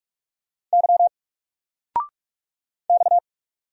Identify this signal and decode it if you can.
Morse code, heard as a keyed tone, about 46 wpm, 700 Hz. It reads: Y X
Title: Single Letter-Number 45wpm